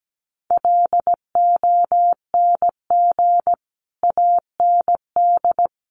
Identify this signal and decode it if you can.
Morse code, heard as a keyed tone, about 17 wpm, 700 Hz. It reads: LONG AND
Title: Sentences from Top 400 Words 17wpm